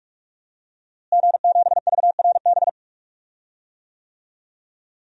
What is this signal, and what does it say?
Morse code, heard as a keyed tone, about 46 wpm, 700 Hz. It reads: G7VRB
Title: DXCC-100 Call Signs - 5 Characters - Repeated 3x - 45wpm